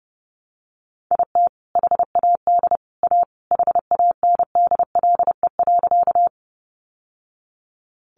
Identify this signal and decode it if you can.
Morse code, heard as a keyed tone, about 30 wpm, 700 Hz. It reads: IT HAD A HANDLE.
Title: Sentences from Top 1100 Words 30wpm